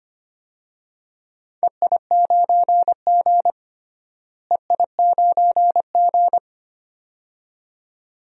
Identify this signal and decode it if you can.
Morse code, heard as a keyed tone, about 25 wpm, 700 Hz. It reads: EI9G EI9G
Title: DXCC-100 Call Signs - 4 Characters - Repeated 2x - 25wpm